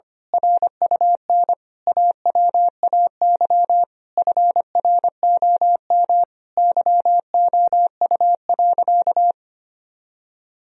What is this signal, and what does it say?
Morse code, heard as a keyed tone, about 25 wpm, 700 Hz. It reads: RUN AWAY FROM YOU.
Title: Sentences from Top 1000 Words - Review Part 1 - 25wpm